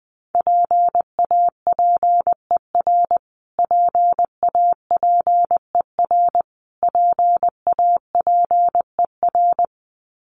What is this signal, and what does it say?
Morse code, heard as a keyed tone, about 20 wpm, 700 Hz. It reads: PAPER PAPER PAPER
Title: Top 300 Words Repeated 20wpm